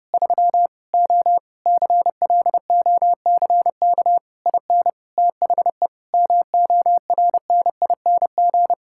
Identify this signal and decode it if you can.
Morse code, heard as a keyed tone, about 30 wpm, 700 Hz. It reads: 3 O CLOCK IN THE MORNING
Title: Pangram Challenge v2 - 30wpm